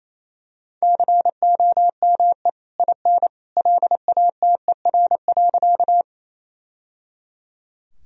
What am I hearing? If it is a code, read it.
Morse code, heard as a keyed tone, about 28 wpm, 700 Hz. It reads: COME IN LATER.
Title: Sentences from Top 1000 Words - Review Part 4 - 28wpm